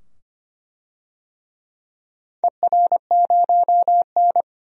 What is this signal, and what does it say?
Morse code, heard as a keyed tone, about 25 wpm, 700 Hz. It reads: ER0N
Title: DXCC-100 Call Signs - 4 Characters - 25wpm